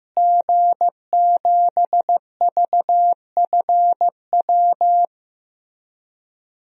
Morse code, heard as a keyed tone, about 15 wpm, 700 Hz. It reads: G7VFW
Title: DXCC-100 Call Signs - 5 Characters - Repeated 3x - 15wpm